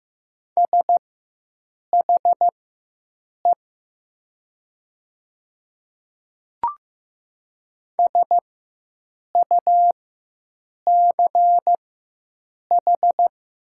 Morse code, heard as a keyed tone, about 15 wpm, spaced as for 7 wpm, 700 Hz. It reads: SHE SUCH
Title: 4x Character Spacing - Top 200 Words - 15wpm